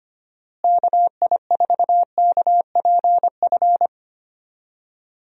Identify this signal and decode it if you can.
Morse code, heard as a keyed tone, about 25 wpm, 700 Hz. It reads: KI4KPF
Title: USA Call Signs - 6 Characters - Repeated 3x - 25wpm